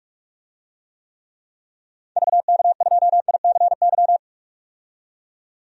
Morse code, heard as a keyed tone, about 44 wpm, 690 Hz. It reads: VK2ICY
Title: DXCC-100 Call Signs - 6 Characters - 45wpm